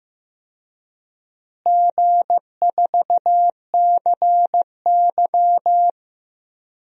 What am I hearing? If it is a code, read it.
Morse code, heard as a keyed tone, about 15 wpm, 700 Hz. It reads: G4CY
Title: DXCC-100 Call Signs - 4 Characters - Repeated 2x - 15wpm